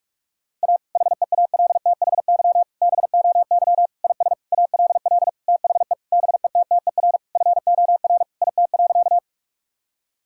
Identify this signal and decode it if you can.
Morse code, heard as a keyed tone, about 46 wpm, 700 Hz. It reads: A HEALTHY BOY IS ALL THE BETTER FOR IT.
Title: Sentences from Top 1000 Words - Review Part 5 - 45wpm